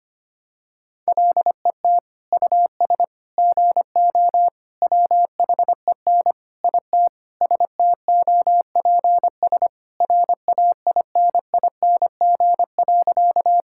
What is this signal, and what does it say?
Morse code, heard as a keyed tone, about 25 wpm, 700 Hz. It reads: LET US GO WHEN IT STOPS RAINING.
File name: practice / 25wpm / Sentences with 2-Word Phrases - 25wpm